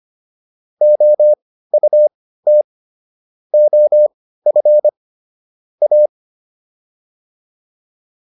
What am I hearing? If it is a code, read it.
Morse code, heard as a keyed tone, about 25 wpm, spaced as for 15 wpm, 600 Hz. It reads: OUT OF A